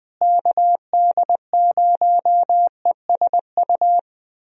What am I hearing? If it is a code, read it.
Morse code, heard as a keyed tone, about 20 wpm, 700 Hz. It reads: KD0ESU
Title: USA Call Signs - 6 Characters - Repeated 3x - 20wpm